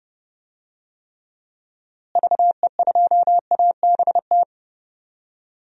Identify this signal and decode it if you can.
Morse code, heard as a keyed tone, about 30 wpm, 700 Hz. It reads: VE2ABT
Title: DXCC-100 Call Signs - 6 Characters - Repeated 3x - 30wpm